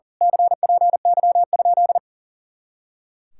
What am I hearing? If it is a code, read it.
Morse code, heard as a keyed tone, about 41 wpm, 700 Hz. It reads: CPY?
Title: Speed-Racing Common QSO Phrases - 40wpm